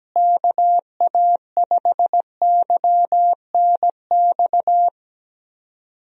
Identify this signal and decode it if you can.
Morse code, heard as a keyed tone, about 17 wpm, 700 Hz. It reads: KA5YNX
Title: USA Call Signs - 6 Characters - Repeated 2x - 17wpm